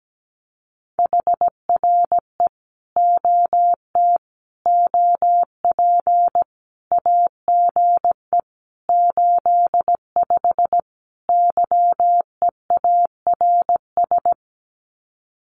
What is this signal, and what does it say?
Morse code, heard as a keyed tone, about 17 wpm, 700 Hz. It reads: HRE OT OP AGE 85 YEARS
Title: Common QSO Phrases 17wpm